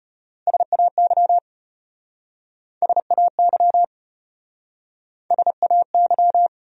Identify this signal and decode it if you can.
Morse code, heard as a keyed tone, about 33 wpm, 700 Hz. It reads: SAY SAY SAY